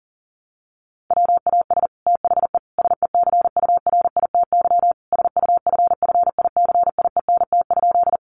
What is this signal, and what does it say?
Morse code, heard as a keyed tone, about 40 wpm, 700 Hz. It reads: WAS THE SECURITY SUFFICIENT?
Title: Sentences from Top 1400 Words 40wpm